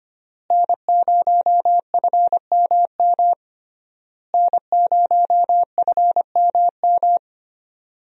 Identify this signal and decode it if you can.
Morse code, heard as a keyed tone, about 25 wpm, 700 Hz. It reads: N0FMM N0FMM
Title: USA Call Signs - 5 Characters - Repeated 3x - 25wpm